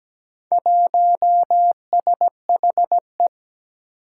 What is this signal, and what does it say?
Morse code, heard as a keyed tone, about 17 wpm, 700 Hz. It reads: 1SHE
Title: Four Challenging Letter-Number Combinations 17wpm